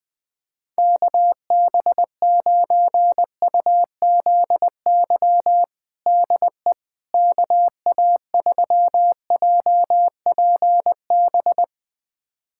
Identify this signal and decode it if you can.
Morse code, heard as a keyed tone, about 20 wpm, 700 Hz. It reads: KB9UZY DE KA3JPB
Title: USA Callsign DE Callsign - 6 Characters - 20wpm